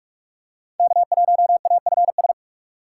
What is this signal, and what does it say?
Morse code, heard as a keyed tone, about 46 wpm, 700 Hz. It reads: K1AFS
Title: USA Call Signs - 5 Characters - 3x Character Spacing - 45wpm